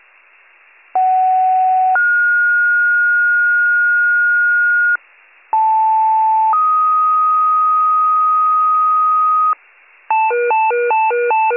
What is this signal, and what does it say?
Two-tone page, heard as a one-second tone, then a three-second tone, about 750 then 1430 Hz, about 880 then 1250 Hz, with noise. It is heard from 0.9 to 9.5 s.